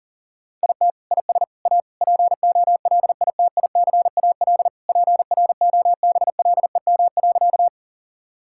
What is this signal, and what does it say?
Morse code, heard as a keyed tone, about 40 wpm, 700 Hz. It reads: IT IS A POLITICAL PROBLEM.
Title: Sentences from Top 1000 Words - Review Part 5 - 40wpm